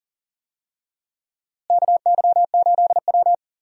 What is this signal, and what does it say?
Morse code, heard as a keyed tone, about 37 wpm, 700 Hz. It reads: KY8W
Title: USA Call Signs - 4 Characters - Repeated 3x - 40wpm